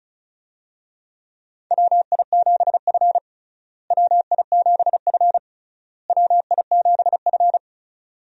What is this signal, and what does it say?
Morse code, heard as a keyed tone, about 36 wpm, 700 Hz. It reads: WI7F WI7F WI7F
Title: USA Call Signs - 4 Characters - Repeated 3x - 35wpm